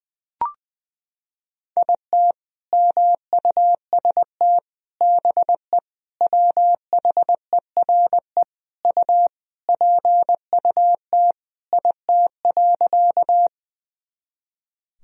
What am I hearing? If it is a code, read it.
Morse code, heard as a keyed tone, about 21 wpm, 700 Hz. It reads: IT MUST BE WHERE U PUT IT.
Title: Sentences from Top 200 Words and Abbreviations 20wpm